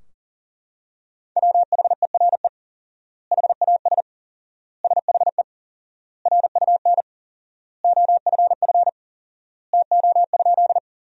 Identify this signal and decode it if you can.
Morse code, heard as a keyed tone, about 40 wpm, 700 Hz. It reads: WHERE HAS SHE RUN OFF TO?
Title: 4x Word-Spacing - Sentences from Top 300 Words - 40wpm